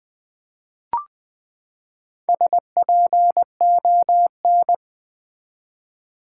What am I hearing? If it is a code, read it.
Morse code, heard as a keyed tone, about 20 wpm, 700 Hz. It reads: SPON
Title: Build A Word - 4 Letters At A Time - 11-Letter Words - 20wpm